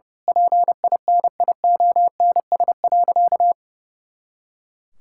Morse code, heard as a keyed tone, about 30 wpm, 700 Hz. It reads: PINIONS.
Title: Pangram Challenge v2 - 30wpm